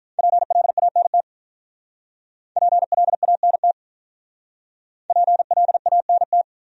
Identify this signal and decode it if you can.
Morse code, heard as a keyed tone, about 48 wpm, 700 Hz. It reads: PLANT PLANT PLANT